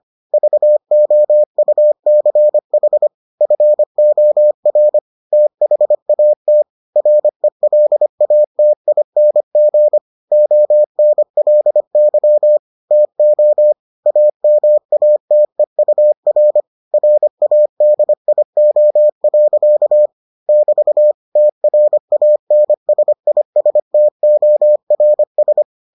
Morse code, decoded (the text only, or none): VOUCH FOR THAT RELATING ONLY TO AMATEUR RADIO. <BT> TRANSISTORS